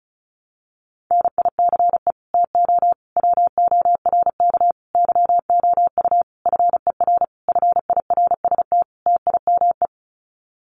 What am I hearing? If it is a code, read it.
Morse code, heard as a keyed tone, about 34 wpm, 700 Hz. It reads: NICE TO WORK YOU FER FIRST TIME
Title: Common QSO Phrases 35wpm